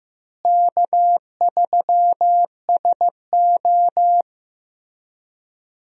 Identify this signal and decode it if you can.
Morse code, heard as a keyed tone, about 15 wpm, 700 Hz. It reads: K3SO
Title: USA Call Signs - 4 Characters - Repeated 3x - 15wpm